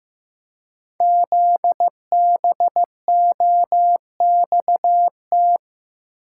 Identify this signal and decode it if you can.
Morse code, heard as a keyed tone, about 15 wpm, 700 Hz. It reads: ZBOXT
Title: Code Groups - 5 Letters - 15wpm